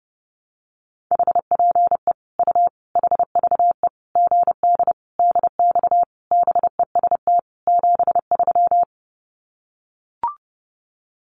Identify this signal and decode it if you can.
Morse code, heard as a keyed tone, about 30 wpm, 700 Hz. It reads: HPE U HVE GD DX BEST 73
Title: Common QSO Phrases 30wpm